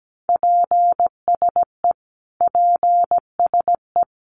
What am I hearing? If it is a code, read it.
Morse code, heard as a keyed tone, about 17 wpm, 700 Hz. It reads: PSE PSE
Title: Abbreviations Repeated 3x - 17wpm